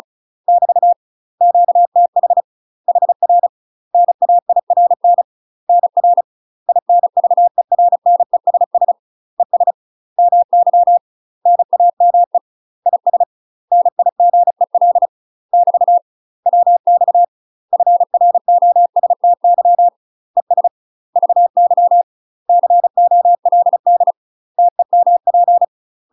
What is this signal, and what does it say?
Morse code, heard as a keyed tone, about 36 wpm, 700 Hz. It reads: <BT> QTH HR NAIRN NR INVERNESS ES MY NAME IS NIGEL <BT> WX FROSTY ES VY COLD TEMP
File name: UK MASTER QSOs TRAINING/35WPM